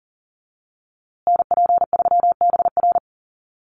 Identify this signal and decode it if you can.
Morse code, heard as a keyed tone, about 40 wpm, 700 Hz. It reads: NP3BR
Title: USA Call Signs Repeated 2x 40wpm